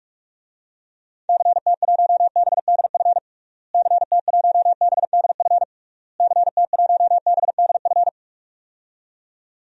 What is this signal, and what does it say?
Morse code, heard as a keyed tone, about 46 wpm, 700 Hz. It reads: CT1BDF CT1BDF CT1BDF
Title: DXCC-100 Call Signs - 6 Characters - Repeated 3x - 45wpm